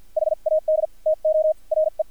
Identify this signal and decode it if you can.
Morse code, heard as a keyed tone, about 50 wpm, spaced as for 42 wpm, 630 Hz. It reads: HAD TO RE
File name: archived / CRSnet